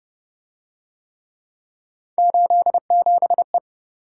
Morse code, heard as a keyed tone, about 25 wpm, 700 Hz. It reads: 87E